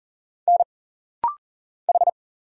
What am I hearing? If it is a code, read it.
Morse code, heard as a keyed tone, about 40 wpm, 700 Hz. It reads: N H
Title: Single Letter-Number Alt - Rapid-Fire 40wpm